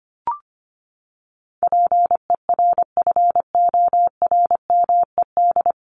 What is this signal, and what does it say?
Morse code, heard as a keyed tone, about 25 wpm, 700 Hz. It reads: PERFORMED
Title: ED Suffix - Top 100 Words with Base Words 25wpm